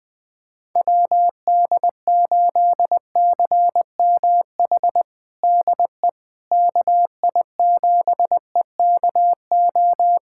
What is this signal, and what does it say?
Morse code, heard as a keyed tone, about 20 wpm, 700 Hz. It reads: WD8CMH DE KI7EKO
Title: USA Callsign DE Callsign 20wpm